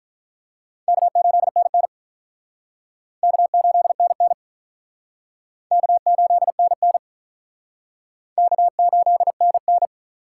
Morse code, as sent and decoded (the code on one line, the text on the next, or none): -.- ---.. -. -. / -.- ---.. -. -. / -.- ---.. -. -. / -.- ---.. -. -.
K8NN K8NN K8NN K8NN